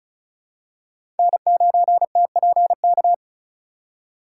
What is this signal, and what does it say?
Morse code, heard as a keyed tone, about 34 wpm, 700 Hz. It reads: N9TPK
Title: USA Call Signs - 5 Characters - 2x Character Spacing - 35wpm